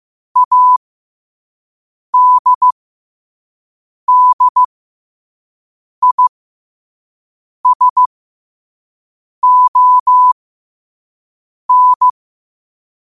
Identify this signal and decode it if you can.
Morse code, heard as a keyed tone, about 15 wpm, spaced as for 5 wpm, 1000 Hz. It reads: ADDISON
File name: qotdfiles / 5wpm